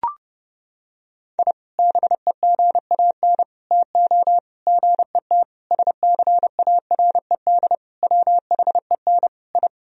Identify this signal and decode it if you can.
Morse code, heard as a keyed tone, about 30 wpm, 700 Hz. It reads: I BEGAN TO GET SCARED WHEN I
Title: Sentences from Top 1000 Words - Review Part 4 - 30wpm